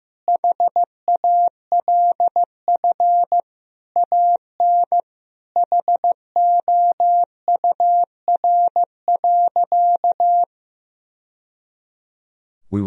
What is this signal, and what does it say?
Morse code, heard as a keyed tone, about 15 wpm, 700 Hz. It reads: HALF AN HOUR.
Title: Sentences from Top 1000 Words - Review Part 1 - 15wpm